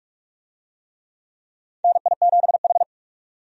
Morse code, heard as a keyed tone, about 43 wpm, 700 Hz. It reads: NI7H
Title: USA Call Signs - 4 Characters - 45wpm